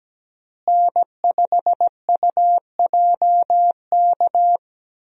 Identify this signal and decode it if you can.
Morse code, heard as a keyed tone, about 17 wpm, 700 Hz. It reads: N5UJK
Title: USA Call Signs - 5 Characters - Repeated 3x - 17wpm